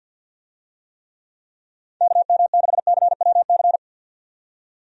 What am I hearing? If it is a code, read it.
Morse code, heard as a keyed tone, about 53 wpm, 700 Hz. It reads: KM6CWC
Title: USA Call Signs - 6 Characters - 50wpm